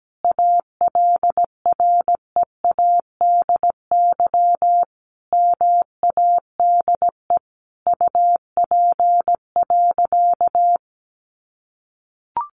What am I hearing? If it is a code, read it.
Morse code, heard as a keyed tone, about 17 wpm, 700 Hz. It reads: ALREADY MADE UP.
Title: Sentences from Top 700 Words 17wpm